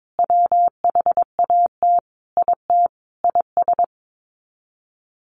Morse code, heard as a keyed tone, about 22 wpm, 700 Hz. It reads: WHAT IT IS
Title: Sets of 5 Words Encore 22wpm